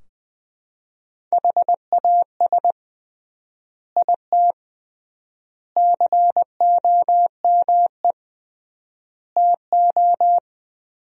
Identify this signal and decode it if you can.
Morse code, heard as a keyed tone, about 21 wpm, 700 Hz. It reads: HAS IT COME TO
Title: 3x Word-Spacing - Sentences from Top 100 Words - 20wpm